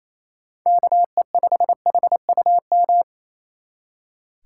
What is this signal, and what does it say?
Morse code, heard as a keyed tone, about 33 wpm, 700 Hz. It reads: KE5HUM
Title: Speed Racing - USA 2x3 Call Signs 28wpm